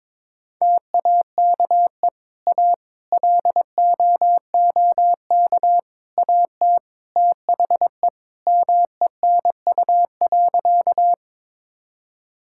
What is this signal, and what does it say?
Morse code, heard as a keyed tone, about 21 wpm, 700 Hz. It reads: TAKE A LOOK AT THE MENU.
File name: practice / 22wpm / Sentences with 4-Word Phrases - 22wpm